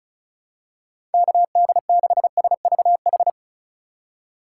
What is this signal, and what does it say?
Morse code, heard as a keyed tone, about 36 wpm, 700 Hz. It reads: KD6SVH
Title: USA Call Signs - 3x Character Spacing - 35wpm